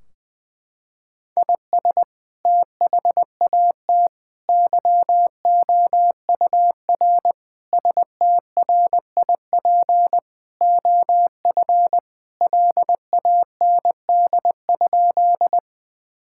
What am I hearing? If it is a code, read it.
Morse code, heard as a keyed tone, about 20 wpm, 700 Hz. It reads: IS THAT YOUR STRIP OF LAND?
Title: Sentences from Top 1600 Words 20wpm